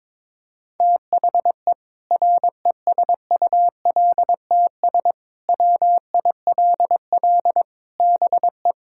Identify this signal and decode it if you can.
Morse code, heard as a keyed tone, about 22 wpm, 700 Hz. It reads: THE RESULTS WILL BE
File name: Sentences with 7-Word Phrases - 22wpm